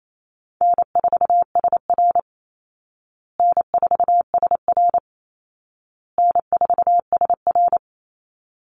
Morse code, heard as a keyed tone, about 28 wpm, 700 Hz. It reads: N4SR N4SR N4SR
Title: Call Signs Repeated 2x - 28wpm